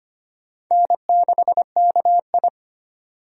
Morse code, heard as a keyed tone, about 25 wpm, 700 Hz. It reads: N6KI
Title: USA Call Signs - 4 Characters - 25wpm